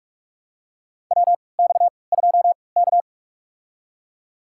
Morse code, heard as a keyed tone, about 42 wpm, spaced as for 25 wpm, 700 Hz. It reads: WX2K
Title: USA Call Signs - 4 Characters - 3x Character Spacing - 45wpm